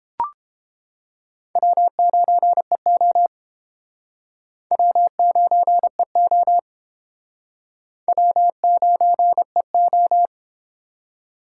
Morse code, heard as a keyed tone, about 30 wpm, 700 Hz. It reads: W9EO W9EO W9EO